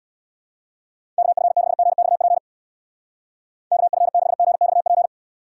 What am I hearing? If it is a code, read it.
Morse code, heard as a keyed tone, about 60 wpm, 700 Hz. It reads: KF6GXP KF6GXP
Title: Speed Racing - USA Call Signs 50wpm